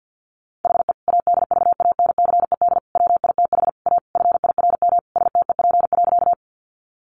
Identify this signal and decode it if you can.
Morse code, heard as a keyed tone, about 50 wpm, 700 Hz. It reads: HE ADVANCED WITH A FIRM STEP.
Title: Sentences from Top 1400 Words 50wpm